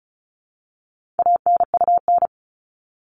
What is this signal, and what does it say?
Morse code, heard as a keyed tone, about 34 wpm, 700 Hz. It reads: ANUN
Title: All Cut Numbers - 4-Digit 35wpm